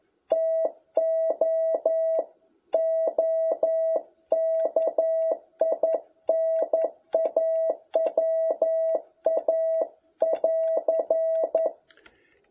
Morse code, heard as a keyed tone, about 11 wpm, 650 Hz. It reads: TO OKINAWA<AR>
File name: jh8cbh / 2021cw